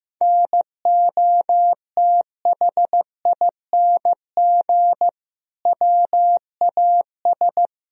Morse code, heard as a keyed tone, about 15 wpm, 700 Hz. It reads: NOTHING WAS
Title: Sentences from Top 1000 Words - Review Part 2 - 15wpm